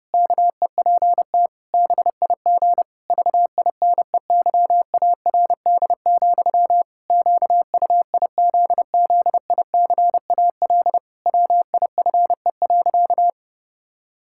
Morse code, heard as a keyed tone, about 32 wpm, 700 Hz. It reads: KEPT BIG VINEYARD, QUIZZICAL WIFE.
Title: Pangram Challenge v2 - 30wpm